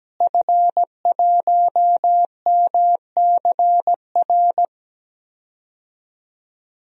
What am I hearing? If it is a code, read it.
Morse code, heard as a keyed tone, about 17 wpm, 700 Hz. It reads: F1MCR
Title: DXCC-100 Call Signs - 5 Characters - Repeated 2x - 17wpm